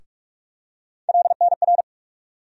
Morse code, heard as a keyed tone, about 44 wpm, 700 Hz. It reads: LNR
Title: USA Call Sign Postfixes - 3 Letters 45wpm